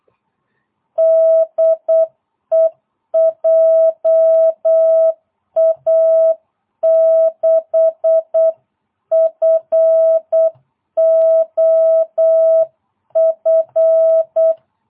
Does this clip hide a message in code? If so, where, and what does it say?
Yes: Morse code from 1.0 to 14.5 s, heard as a keyed tone, about 8 wpm, 650 Hz. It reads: DE JA6FOF